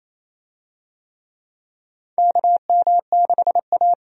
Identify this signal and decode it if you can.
Morse code, heard as a keyed tone, about 28 wpm, 700 Hz. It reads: KM6A
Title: USA Call Signs - 4 Characters - 28wpm